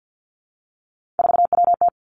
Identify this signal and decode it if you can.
Morse code, heard as a keyed tone, about 50 wpm, 700 Hz. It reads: 4WT
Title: Three Challenging Letter-Number Combinations 50wpm